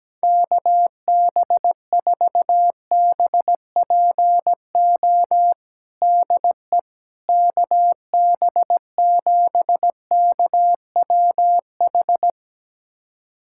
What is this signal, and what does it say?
Morse code, heard as a keyed tone, about 17 wpm, 700 Hz. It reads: KB4BPO DE KB7KWH
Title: USA Callsign DE Callsign - 6 Characters - 17wpm